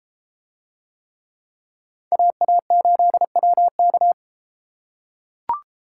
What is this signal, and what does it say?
Morse code, heard as a keyed tone, about 33 wpm, 700 Hz. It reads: AA8WK